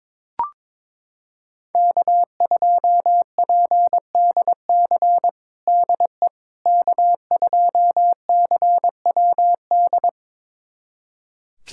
Morse code, heard as a keyed tone, about 22 wpm, 700 Hz. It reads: K2PDC DE K2CWD
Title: USA Callsign DE Callsign - 5 Characters - 22wpm